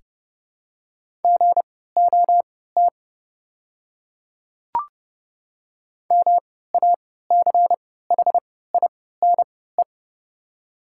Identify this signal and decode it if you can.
Morse code, heard as a keyed tone, about 30 wpm, spaced as for 17 wpm, 700 Hz. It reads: GOT MACHINE
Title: 3x Character Spacing - Top 500 Words - 30wpm